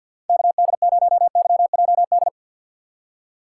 Morse code, heard as a keyed tone, about 50 wpm, 700 Hz. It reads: KD0YJD
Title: USA Call Signs - 2x Character Spacing - 50wpm